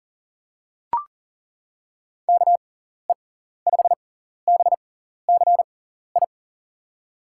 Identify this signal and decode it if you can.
Morse code, heard as a keyed tone, about 40 wpm, spaced as for 14 wpm, 700 Hz. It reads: KE5BCI
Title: USA Call Signs - 6 Characters - 6x Character Spacing - 40wpm